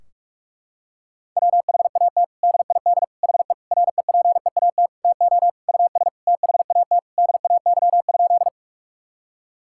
Morse code, heard as a keyed tone, about 46 wpm, 700 Hz. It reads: WHAT DID HE REPEAT TO US THAT DAY?
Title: Sentences from Top 1600 Words 45wpm